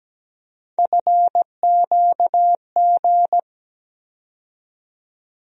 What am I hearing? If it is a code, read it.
Morse code, heard as a keyed tone, about 17 wpm, 700 Hz. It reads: FQG
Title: USA Call Sign Postfixes - 3 Letters 17wpm